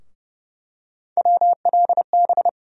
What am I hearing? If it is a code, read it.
Morse code, heard as a keyed tone, about 29 wpm, 700 Hz. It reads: WLB
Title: USA Call Sign Postfixes - 3 Letters 30wpm